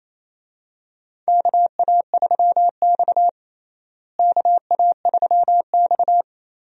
Morse code, heard as a keyed tone, about 28 wpm, 700 Hz. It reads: KA3X KA3X
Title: USA Call Signs - 4 Characters - Repeated 3x - 28wpm